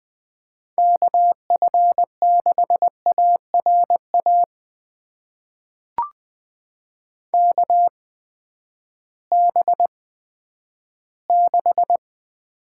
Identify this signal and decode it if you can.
Morse code, heard as a keyed tone, about 20 wpm, spaced as for 11 wpm, 700 Hz. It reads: KF6ARA KB6
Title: USA Call Signs - 6 Characters - 8x Character Spacing - 20wpm